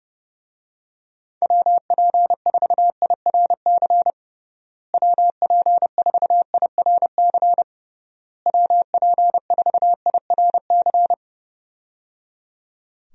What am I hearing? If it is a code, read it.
Morse code, heard as a keyed tone, about 30 wpm, 700 Hz. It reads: WP4IRC WP4IRC WP4IRC
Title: USA Call Signs - 6 Characters - Repeated 3x - 30wpm